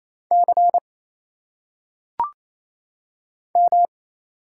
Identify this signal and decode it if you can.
Morse code, heard as a keyed tone, about 28 wpm, 700 Hz. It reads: C M
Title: Single Letters - Phonetic Pronunciation 28wpm